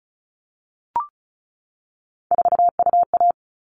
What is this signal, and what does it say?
Morse code, heard as a keyed tone, about 34 wpm, 700 Hz. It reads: 4UA
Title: All Cut Numbers - 3-Digit 35wpm